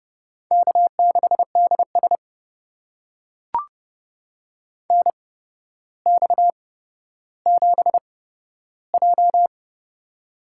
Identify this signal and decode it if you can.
Morse code, heard as a keyed tone, about 31 wpm, spaced as for 13 wpm, 700 Hz. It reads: K6DS NX7J
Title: USA Call Signs - 4 Characters - 8x Character Spacing - 30wpm